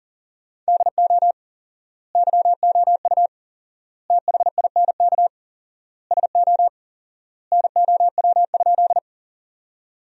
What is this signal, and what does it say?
Morse code, heard as a keyed tone, about 37 wpm, 700 Hz. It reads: DO YOU THINK SO NOW?
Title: 4x Word-Spacing - Sentences from Top 200 Words - 40wpm